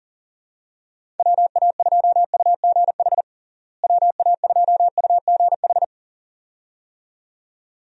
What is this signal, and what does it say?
Morse code, heard as a keyed tone, about 40 wpm, 700 Hz. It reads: WA2UGH WA2UGH
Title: USA Call Signs - 6 Characters - Repeated 2x - 40wpm